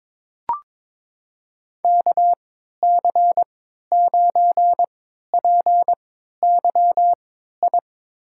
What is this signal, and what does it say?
Morse code, heard as a keyed tone, about 21 wpm, spaced as for 12 wpm, 700 Hz. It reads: KC9PYI
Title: USA Call Signs - 6 Characters - 3x Character Spacing - 22wpm